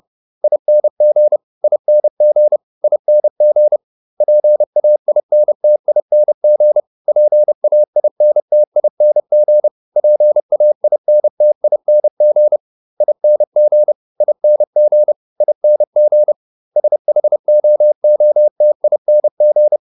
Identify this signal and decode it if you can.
Morse code, heard as a keyed tone, about 30 wpm, 600 Hz. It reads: ING ING ING PAINTING PAINTING PAINTING ING ING ING SHOOTING